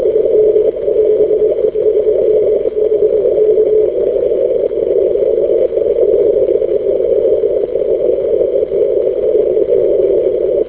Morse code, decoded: none